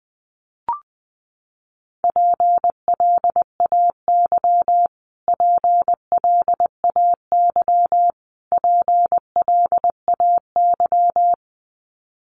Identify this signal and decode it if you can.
Morse code, heard as a keyed tone, about 20 wpm, 700 Hz. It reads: PLAY PLAY PLAY
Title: Top 500 Words Repeated 20wpm